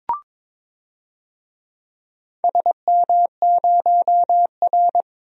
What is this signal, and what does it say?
Morse code, heard as a keyed tone, about 22 wpm, 700 Hz. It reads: SM0R
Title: DXCC-100 Call Signs - 4 Characters - 22wpm